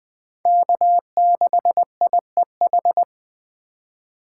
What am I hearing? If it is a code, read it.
Morse code, heard as a keyed tone, about 21 wpm, 700 Hz. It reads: K6IEH